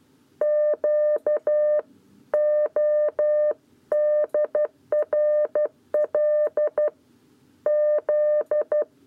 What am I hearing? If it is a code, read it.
Morse code, heard as a keyed tone, about 12 wpm, 600 Hz. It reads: Q ODRL Z
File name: 7b lezione cw 1